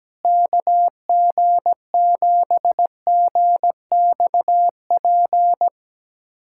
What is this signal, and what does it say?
Morse code, heard as a keyed tone, about 17 wpm, 700 Hz. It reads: KG7GXP
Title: USA Callsign DE Callsign - 6 Characters - 17wpm